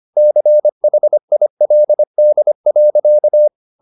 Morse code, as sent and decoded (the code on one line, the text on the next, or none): -.-. .... .. .-.. -.. .-.-.-
CHILD.